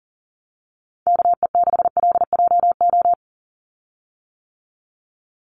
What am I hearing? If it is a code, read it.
Morse code, heard as a keyed tone, about 40 wpm, 700 Hz. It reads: KE6LJO
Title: USA Call Signs 40wpm